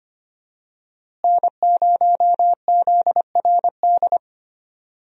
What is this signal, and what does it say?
Morse code, heard as a keyed tone, about 25 wpm, 700 Hz. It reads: N0ZRD
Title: USA Call Signs - 5 Characters - Repeated 2x - 25wpm